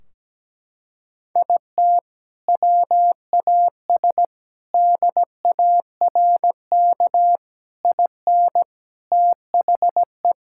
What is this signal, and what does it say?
Morse code, heard as a keyed tone, about 17 wpm, 700 Hz. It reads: IT WAS DARK IN THE
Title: Sentences from Top 1600 Words 17wpm